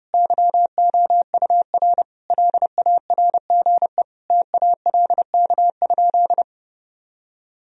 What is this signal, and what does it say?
Morse code, heard as a keyed tone, about 32 wpm, 700 Hz. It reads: YOUR LARGE TALK?
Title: Sentences from Top 1000 Words - Review Part 1 - 30wpm